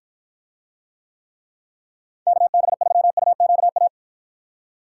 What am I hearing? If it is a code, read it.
Morse code, heard as a keyed tone, about 49 wpm, 700 Hz. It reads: KB3UQA